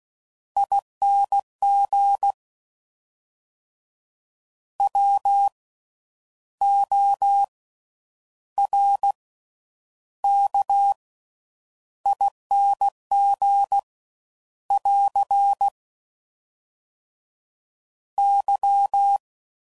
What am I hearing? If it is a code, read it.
Morse code, heard as a keyed tone, about 16 wpm, spaced as for 9 wpm, 800 Hz. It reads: ING WORKING <AR> Y
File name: Slow With Faster Ings